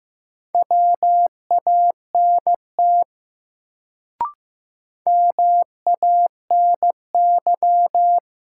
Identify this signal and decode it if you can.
Morse code, heard as a keyed tone, about 15 wpm, 700 Hz. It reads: WANT MANY
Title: Sets of 2 Words - New Word Drill - 15wpm